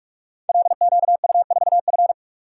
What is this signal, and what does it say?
Morse code, heard as a keyed tone, about 44 wpm, 700 Hz. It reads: LQU4F
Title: Code Groups - 5 Letter:Numbers - 45wpm